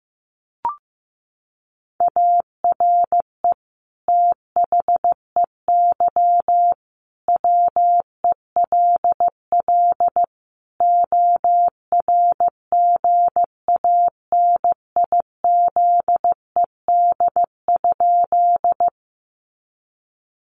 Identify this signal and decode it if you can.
Morse code, heard as a keyed tone, about 15 wpm, 700 Hz. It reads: ARE THEY WELL ORGANIZED?
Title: Sentences from Top 1300 Words 15wpm